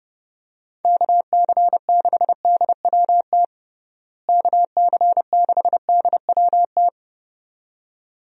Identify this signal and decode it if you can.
Morse code, heard as a keyed tone, about 30 wpm, 700 Hz. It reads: KC6DWT KC6DWT
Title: USA Call Signs - 6 Characters - Repeated 2x - 30wpm